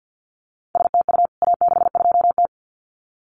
Morse code, heard as a keyed tone, about 50 wpm, 700 Hz. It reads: STU A62T
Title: CWT Exchange 50wpm